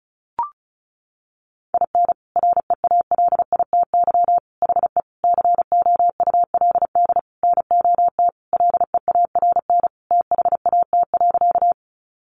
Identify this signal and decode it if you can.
Morse code, heard as a keyed tone, about 34 wpm, 700 Hz. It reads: IN REALITY HE COULD NOT LEARN THAT.
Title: Sentences from Top 1200 Words 35wpm